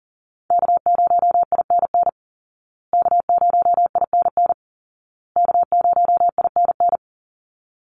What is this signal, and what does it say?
Morse code, heard as a keyed tone, about 40 wpm, 700 Hz. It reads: K0INN K0INN K0INN
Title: Call Signs Repeated 2x - 40wpm